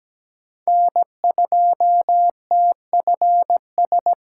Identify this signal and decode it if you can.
Morse code, heard as a keyed tone, about 17 wpm, 700 Hz. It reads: N2TFS
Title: USA Call Signs - 5 Characters - 2x Character Spacing - 17wpm